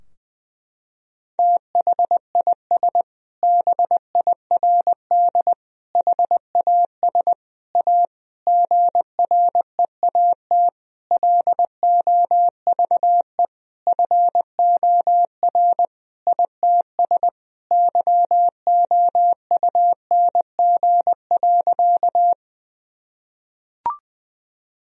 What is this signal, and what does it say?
Morse code, heard as a keyed tone, about 21 wpm, 700 Hz. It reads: THIS BIRD HAS A GREAT LOVE FOR ITS YOUNG.
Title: Sentences from Top 1000 Words - Review Part 2 - 20wpm